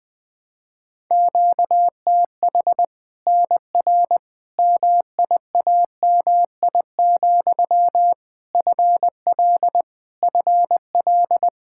Morse code, heard as a keyed tone, about 21 wpm, 700 Hz. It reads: QTH NR MIAMI, FL FL
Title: Practice QSO - Exchange 1 - Beginning - 20wpm